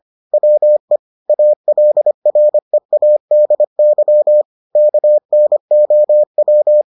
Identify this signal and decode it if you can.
Morse code, heard as a keyed tone, about 25 wpm, 600 Hz. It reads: WE ALREADY KNOW